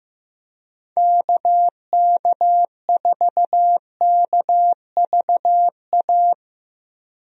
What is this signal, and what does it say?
Morse code, heard as a keyed tone, about 15 wpm, 700 Hz. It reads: KK4KVA
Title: USA Call Signs - 6 Characters - Repeated 2x - 15wpm